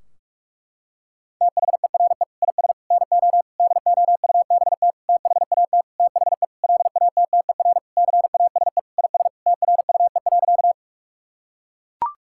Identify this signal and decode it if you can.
Morse code, heard as a keyed tone, about 41 wpm, 700 Hz. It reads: THERE IS NO DOUBT THAT THE LATTER CASE IS TRUE.
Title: Sentences from Top 1600 Words 45wpm